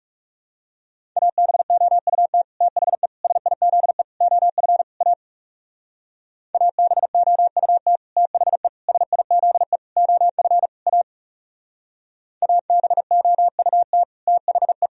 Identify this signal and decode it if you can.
Morse code, heard as a keyed tone, about 39 wpm, 700 Hz. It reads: ABOUT THE SIZE OF A ABOUT THE SIZE OF A ABOUT THE